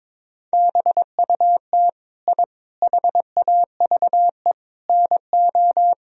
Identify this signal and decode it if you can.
Morse code, heard as a keyed tone, about 22 wpm, 700 Hz. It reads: BUT I HAVE NO
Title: Sentences from Top 1000 Words - Review Part 4 - 22wpm